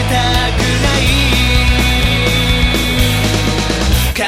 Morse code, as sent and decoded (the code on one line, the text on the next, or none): none